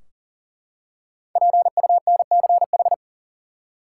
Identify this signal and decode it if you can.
Morse code, heard as a keyed tone, about 40 wpm, 700 Hz. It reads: PUNCH
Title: Five Letter Words - 40wpm